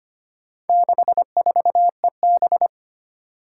Morse code, heard as a keyed tone, about 25 wpm, 700 Hz. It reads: 64EB
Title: Four Challenging Letter-Number Combinations 25wpm